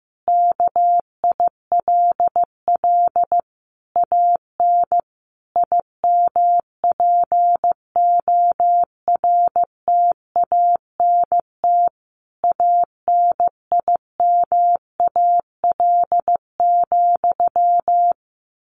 Morse code, heard as a keyed tone, about 15 wpm, 700 Hz. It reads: KILL AN IMPORTANT ANIMAL,
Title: A Sound Of Thunder 15wpm